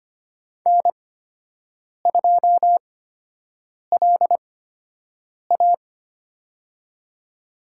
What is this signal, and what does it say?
Morse code, heard as a keyed tone, about 25 wpm, spaced as for 7 wpm, 700 Hz. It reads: N2LA
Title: USA Call Signs - 4 Characters - 8x Character Spacing - 25wpm